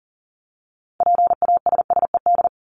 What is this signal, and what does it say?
Morse code, heard as a keyed tone, about 40 wpm, 700 Hz. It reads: PASSED
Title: ED Suffix - Top 100 Words with Base Words 40wpm